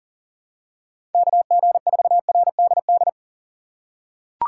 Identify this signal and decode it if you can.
Morse code, heard as a keyed tone, about 40 wpm, 700 Hz. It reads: KG4RDD
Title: USA Call Signs - 6 Characters - 2x Character Spacing - 40wpm